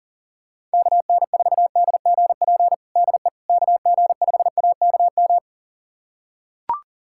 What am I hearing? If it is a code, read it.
Morse code, heard as a keyed tone, about 37 wpm, 700 Hz. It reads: KN4DGP DE KG5AKM
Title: USA Callsign DE Callsign - 6 Characters - 40wpm